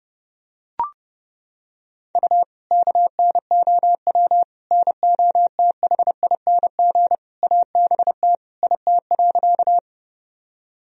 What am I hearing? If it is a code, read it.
Morse code, heard as a keyed tone, about 30 wpm, 700 Hz. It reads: U KNOW NOTHING ABT IT.
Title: Sentences from Top 500 Words and Abbreviations 30wpm